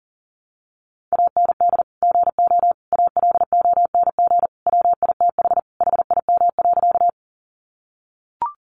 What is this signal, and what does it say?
Morse code, heard as a keyed tone, about 40 wpm, 700 Hz. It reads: AND GO ALONG WITH HIM.
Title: Sentences from Top 700 Words 40wpm